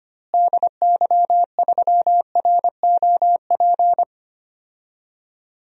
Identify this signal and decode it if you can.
Morse code, heard as a keyed tone, about 25 wpm, 700 Hz. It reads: DY3ROP
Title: DXCC-100 Call Signs - 6 Characters - Repeated 2x - 25wpm